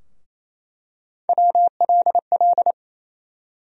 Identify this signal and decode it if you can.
Morse code, heard as a keyed tone, about 27 wpm, 700 Hz. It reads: WLL
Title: USA Call Sign Postfixes - 3 Letters 28wpm